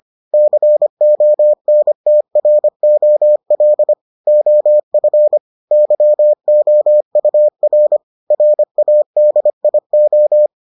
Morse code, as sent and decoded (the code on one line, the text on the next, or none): -.-. --- -. - .-. --- .-.. / --- ..-. / -.-- --- ..- .-. / .-. .- -.. .. ---
CONTROL OF YOUR RADIO